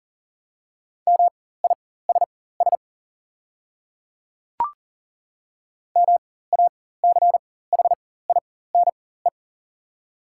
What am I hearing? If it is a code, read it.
Morse code, heard as a keyed tone, about 40 wpm, spaced as for 19 wpm, 700 Hz. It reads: MISS MACHINE
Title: 4x Character Spacing - Top 500 Words - 40wpm